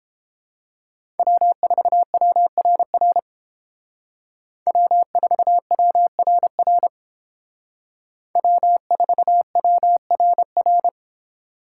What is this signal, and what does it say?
Morse code, heard as a keyed tone, about 29 wpm, 700 Hz. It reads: W4WRR W4WRR W4WRR